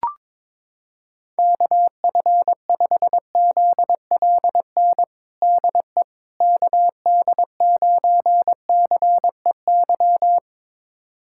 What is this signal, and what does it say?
Morse code, heard as a keyed tone, about 22 wpm, 700 Hz. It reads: KF5ZLN DE KD9CEY
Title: USA Callsign DE Callsign 22wpm